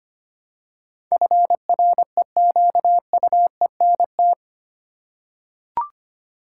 Morse code, heard as a keyed tone, about 25 wpm, 700 Hz. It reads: FREQUENT
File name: Eight Letter Words - 25wpm